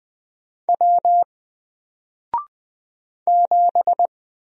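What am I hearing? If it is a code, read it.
Morse code, heard as a keyed tone, about 20 wpm, 700 Hz. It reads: W 7
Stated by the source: Single letter and number Morse drill at 20 wpm, from N4LCD.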